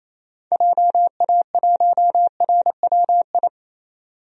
Morse code, heard as a keyed tone, about 27 wpm, 700 Hz. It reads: JA1RWI
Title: DXCC-100 Call Signs - 6 Characters - 28wpm